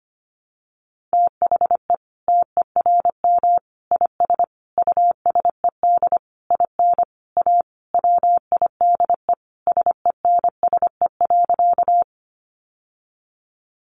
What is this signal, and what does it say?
Morse code, heard as a keyed tone, about 25 wpm, 700 Hz. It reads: THE TERM IS USED IN A WIDE SENSE.
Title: Sentences from Top 900 Words 25wpm